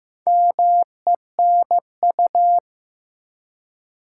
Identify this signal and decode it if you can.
Morse code, heard as a keyed tone, about 15 wpm, 700 Hz. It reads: MENU
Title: Four Letter Words - 15wpm